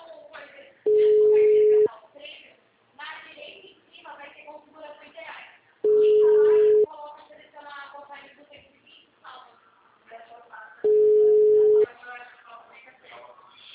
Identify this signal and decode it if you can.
Ringback tone (Europe, 425 Hz), with speech.
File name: out- -2018- - - .46220